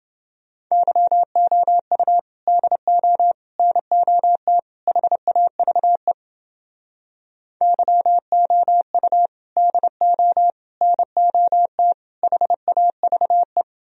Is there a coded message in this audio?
Yes: Morse code from 0.7 to 13.6 s, heard as a keyed tone, about 30 wpm, 700 Hz. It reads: YOU DO NOT HAVE YOU DO NOT HAVE